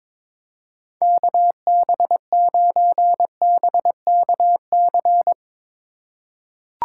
Morse code, heard as a keyed tone, about 22 wpm, 700 Hz. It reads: KB9BKC
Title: USA Call Signs - 6 Characters - 4x Character Spacing - 22wpm